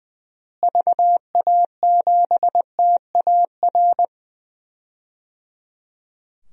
Morse code, heard as a keyed tone, about 20 wpm, 700 Hz. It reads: VA7TAR
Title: DXCC-100 Call Signs - 6 Characters - Repeated 3x - 20wpm